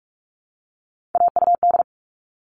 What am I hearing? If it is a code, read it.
Morse code, heard as a keyed tone, about 44 wpm, 700 Hz. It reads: AUD
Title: All Cut Numbers - 3-Digit 45wpm